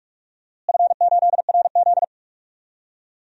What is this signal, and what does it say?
Morse code, heard as a keyed tone, about 46 wpm, 700 Hz. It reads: F8RZ
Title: DXCC-100 Call Signs - 4 Characters - Repeated 3x - 45wpm